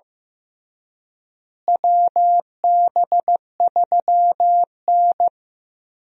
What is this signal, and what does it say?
Morse code, heard as a keyed tone, about 16 wpm, 700 Hz. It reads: WB3N
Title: USA Call Signs - 4 Characters - Repeated 2x - 15wpm